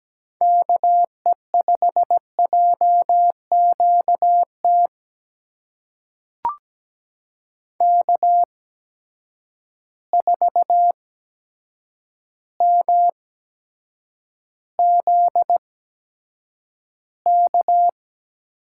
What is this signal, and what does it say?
Morse code, heard as a keyed tone, about 17 wpm, spaced as for 8 wpm, 700 Hz. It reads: KE5JQT K4MZK
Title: USA Call Signs - 8x Character Spacing - 17wpm